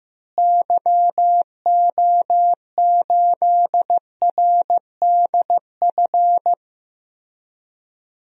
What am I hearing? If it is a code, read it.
Morse code, heard as a keyed tone, about 16 wpm, 700 Hz. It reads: YO8RDF
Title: DXCC-100 Call Signs - 6 Characters - Repeated 2x - 15wpm